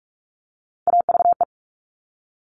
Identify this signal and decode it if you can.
Morse code, heard as a keyed tone, about 46 wpm, 700 Hz. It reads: AVE
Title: Top Three Letter Combinations in Words 45wpm